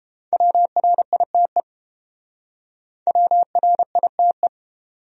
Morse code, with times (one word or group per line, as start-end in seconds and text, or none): 0.3-1.6 WRITE
3.1-4.5 WRITE